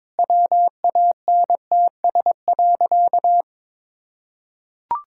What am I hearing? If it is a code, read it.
Morse code, heard as a keyed tone, about 22 wpm, 700 Hz. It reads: WANTS.
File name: practice / 22wpm / Sentences with 2-Word Phrases - 22wpm